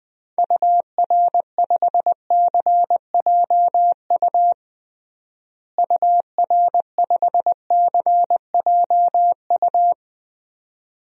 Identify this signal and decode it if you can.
Morse code, heard as a keyed tone, about 21 wpm, 700 Hz. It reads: UR5CJU UR5CJU
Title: DXCC-100 Call Signs - 6 Characters - Repeated 2x - 20wpm